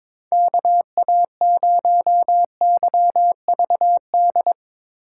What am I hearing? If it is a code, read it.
Morse code, heard as a keyed tone, about 22 wpm, 700 Hz. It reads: KA0YVD
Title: USA Call Signs - 6 Characters - 4x Character Spacing - 22wpm